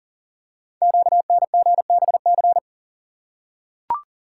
Morse code, heard as a keyed tone, about 41 wpm, 700 Hz. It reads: QNGBC
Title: Code Groups - 5 Letters - 40wpm